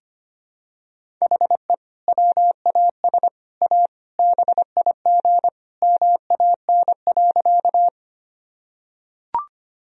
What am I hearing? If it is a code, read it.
Morse code, heard as a keyed tone, about 25 wpm, 700 Hz. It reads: HE WAS A BIG MAN.
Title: Sentences from Top 1000 Words - Review Part 1 - 25wpm